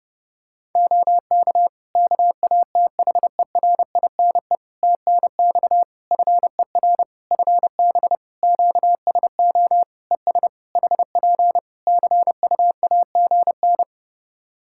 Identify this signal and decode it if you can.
Morse code, heard as a keyed tone, about 32 wpm, 700 Hz. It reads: OK KATHERINE TNX FER FB QSO ES HP CUAGN
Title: K1USN SST QSO Practice 30wpm